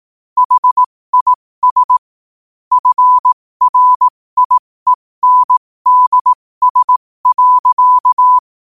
Morse code, heard as a keyed tone, about 18 wpm, spaced as for 15 wpm, 1000 Hz. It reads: HIS FRIENDS.
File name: qotdfiles / 15wpm